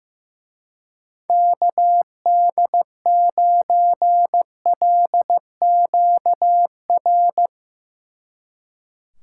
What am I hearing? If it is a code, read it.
Morse code, heard as a keyed tone, about 16 wpm, 700 Hz. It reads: KD9LQR
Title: Speed Racing - USA 2x3 Call Signs 15wpm